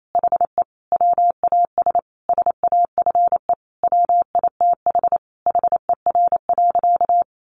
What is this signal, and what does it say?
Morse code, heard as a keyed tone, about 28 wpm, 700 Hz. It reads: HE WAS SAFE WITH HER.
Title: Sentences from Top 900 Words 28wpm